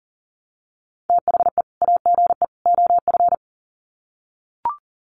Morse code, heard as a keyed tone, about 40 wpm, 700 Hz. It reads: THE AGE OF
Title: Sets of 3 Words 40wpm